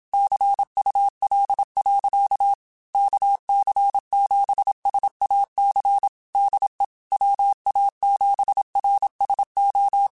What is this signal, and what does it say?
Morse code, heard as a keyed tone, about 27 wpm, 800 Hz. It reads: CUL. KC7SAC DE WA7RSO
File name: QSOs / QSOs-mp3s / kc7sac@25